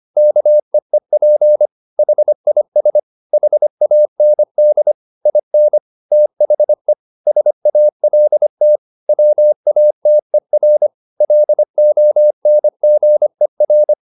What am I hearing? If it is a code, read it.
Morse code, heard as a keyed tone, about 25 wpm, 600 Hz. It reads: KEEP HIS HAND IN THE SALT WATER LONGER
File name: the old man and the sea part 1 25wpm